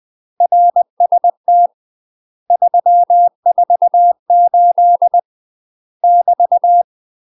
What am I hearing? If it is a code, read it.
Morse code, heard as a keyed tone, about 20 wpm, 700 Hz. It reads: RST 348 <BT>